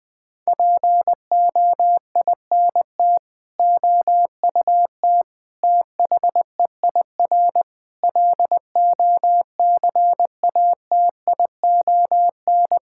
Morse code, heard as a keyed tone, about 20 wpm, 700 Hz. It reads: POINT OUT THEIR LOCATION
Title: Sentences from Top 1000 Words - Review Part 6 - 20wpm